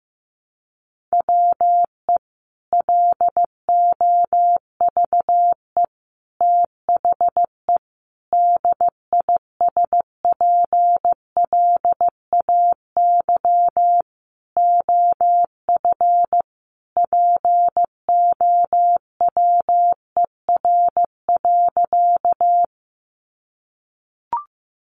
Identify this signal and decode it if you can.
Morse code, heard as a keyed tone, about 15 wpm, 700 Hz. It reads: WE LOVE THE DISPLAY OF POWER.
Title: Sentences from Top 1200 Words 15wpm